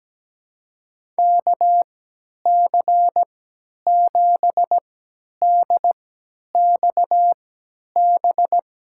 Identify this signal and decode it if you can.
Morse code, heard as a keyed tone, about 17 wpm, spaced as for 10 wpm, 700 Hz. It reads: KC7DXB
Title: USA Call Signs - 6 Characters - 3x Character Spacing - 17wpm